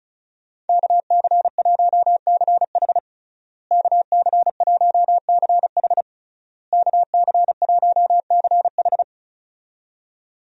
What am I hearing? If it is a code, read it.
Morse code, heard as a keyed tone, about 36 wpm, 700 Hz. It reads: KC1CH KC1CH KC1CH
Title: USA Call Signs - 5 Characters - Repeated 3x - 35wpm